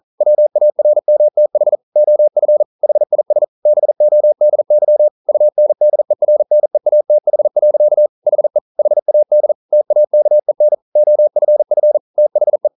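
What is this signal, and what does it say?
Morse code, heard as a keyed tone, about 42 wpm, 600 Hz. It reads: WARMTH OF HIS BODY UNDERNEATH. HE HAD TAKEN OFF THE
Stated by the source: Morse code practice at 41 wpm, from CW med Gnister.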